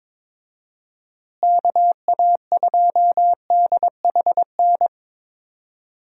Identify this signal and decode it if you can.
Morse code, heard as a keyed tone, about 21 wpm, 700 Hz. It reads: KA2DHN
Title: USA Call Signs - 6 Characters - Repeated 2x - 22wpm